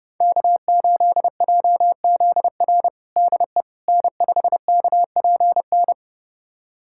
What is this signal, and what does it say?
Morse code, heard as a keyed tone, about 30 wpm, 700 Hz. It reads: K8JZR DE N5KPN
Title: USA Callsign DE Callsign - 5 Characters - 30wpm